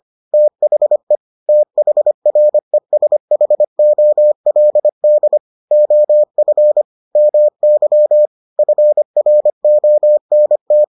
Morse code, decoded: THE THRESHOLD OF MY FRONT

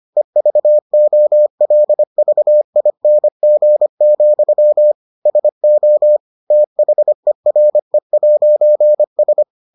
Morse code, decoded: EVOLVING, SO THERE'S

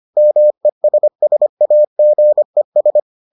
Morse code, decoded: MESSAGES

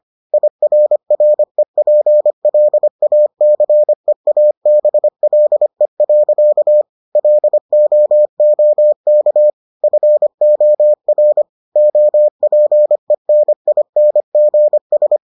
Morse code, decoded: IRREPLACEABLE. LOOK FOR OPENINGS

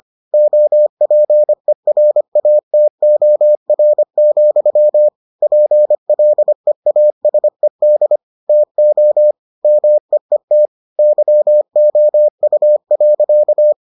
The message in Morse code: OPERATOR, PLEASED TO MEET YOU.

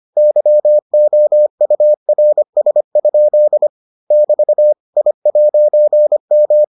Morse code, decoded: YOURS? <BT> I'M

